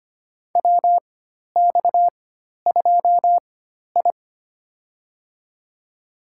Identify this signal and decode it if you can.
Morse code, heard as a keyed tone, about 25 wpm, spaced as for 12 wpm, 700 Hz. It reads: WX2I
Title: USA Call Signs - 4 Characters - 4x Character Spacing - 25wpm